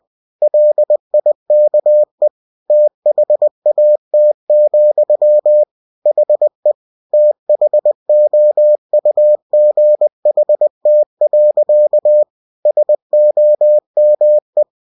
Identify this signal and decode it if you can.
Morse code, heard as a keyed tone, about 20 wpm, 600 Hz. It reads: LIKE THAT, HE THOUGHT. SOME
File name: cw-books / winnie-the-pooh / winnie-20wpm-page-0070